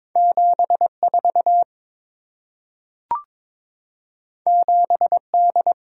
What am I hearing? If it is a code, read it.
Morse code, heard as a keyed tone, about 23 wpm, 690 Hz. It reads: 74 7D
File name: practice / 22wpm / Two Letter-Number Combinations - 22wpm